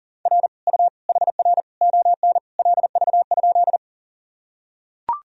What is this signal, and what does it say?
Morse code, heard as a keyed tone, about 37 wpm, 700 Hz. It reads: R U HR ON LV?
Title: Sentences from Top 400 Words and Abbreviations 40wpm